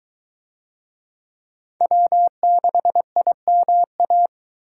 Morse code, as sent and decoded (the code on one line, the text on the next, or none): .-- -.... .. -- .-
W6IMA